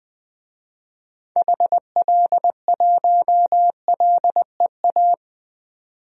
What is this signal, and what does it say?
Morse code, heard as a keyed tone, about 20 wpm, 700 Hz. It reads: HL1LEA
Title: DXCC-100 Call Signs - 6 Characters - 20wpm